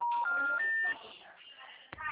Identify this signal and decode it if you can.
Special information tone, with noise.